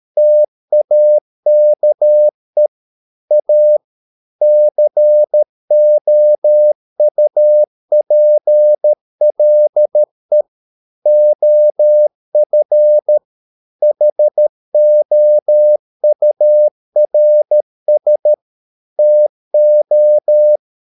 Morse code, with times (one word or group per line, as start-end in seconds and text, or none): 0.2-2.7 TAKE
3.3-3.8 A
4.4-10.4 COUPLE
11.1-13.2 OF
13.8-18.3 HOURS
19.0-20.6 TO